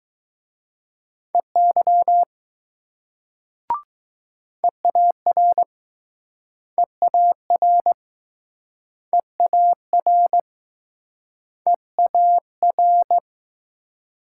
Morse code, heard as a keyed tone, about 20 wpm, 700 Hz. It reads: EY EAR EAR EAR EAR